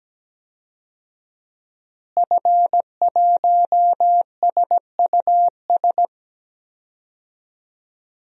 Morse code, heard as a keyed tone, about 17 wpm, 700 Hz. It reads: F1SUS
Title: DXCC-100 Call Signs - 5 Characters - Repeated 3x - 17wpm